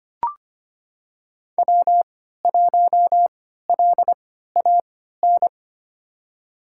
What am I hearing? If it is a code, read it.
Morse code, heard as a keyed tone, about 25 wpm, spaced as for 14 wpm, 700 Hz. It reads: W1LAN
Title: USA Call Signs - 3x Character Spacing - 25wpm